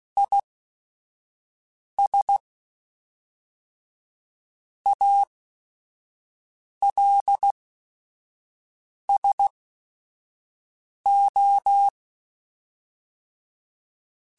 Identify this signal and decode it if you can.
Morse code, heard as a keyed tone, about 16 wpm, spaced as for 5 wpm, 800 Hz. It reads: IS ALSO
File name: Technical Subjects / Technical Subjects-mp3s / XC @5